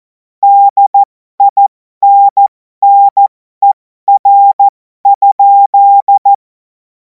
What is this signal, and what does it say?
Morse code, heard as a keyed tone, about 14 wpm, spaced as for 12 wpm, 800 Hz. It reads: DINNER?
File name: Morsekurs / Witze